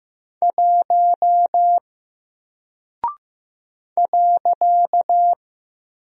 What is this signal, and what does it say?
Morse code, heard as a keyed tone, about 15 wpm, 700 Hz. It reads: 1 .
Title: Single Character (Letter:Number:Punctuation:Prosign) Alt 15wpm